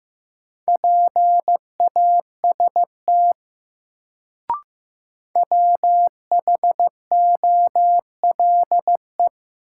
Morse code, heard as a keyed tone, about 16 wpm, 700 Hz. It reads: PAST WHOLE
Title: Sets of 2 Words - New Word Drill - 15wpm